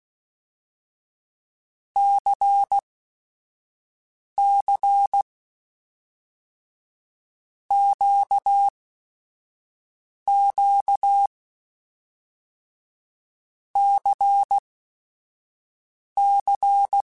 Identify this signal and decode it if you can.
Morse code, heard as a keyed tone, about 16 wpm, spaced as for 5 wpm, 800 Hz. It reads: CC QQ CC
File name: Basic Sets / Basic Letter Groups Collective7 / LFCYQ@5